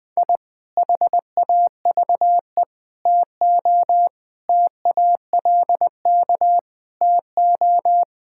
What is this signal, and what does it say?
Morse code, heard as a keyed tone, about 20 wpm, 700 Hz. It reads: I HAVE TO TALK TO
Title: Sentences with 3-Word Phrases - Spoken Beginning - 20wpm